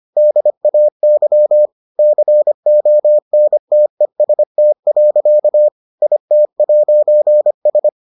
Morse code, heard as a keyed tone, about 25 wpm, 600 Hz. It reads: DAY CONTEST. IT'S